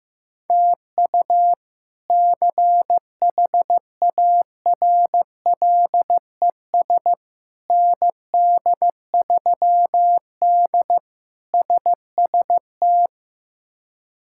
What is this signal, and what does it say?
Morse code, heard as a keyed tone, about 16 wpm, 700 Hz. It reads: TU CHARLES ND3D SST
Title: K1USN SST QSO Practice - 15wpm